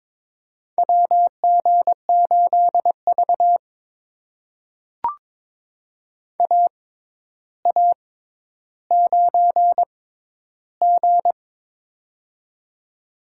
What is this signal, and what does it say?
Morse code, heard as a keyed tone, about 22 wpm, spaced as for 11 wpm, 700 Hz. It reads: WG8V AA9G
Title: USA Call Signs - 4 Characters - 6x Character Spacing - 22wpm